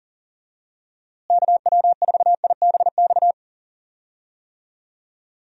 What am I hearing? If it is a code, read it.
Morse code, heard as a keyed tone, about 40 wpm, 700 Hz. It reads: KW4IBX
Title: USA Call Signs - 6 Characters - Repeated 3x - 40wpm